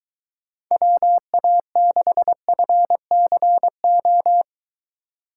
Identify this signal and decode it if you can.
Morse code, heard as a keyed tone, about 22 wpm, 700 Hz. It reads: WA6FCO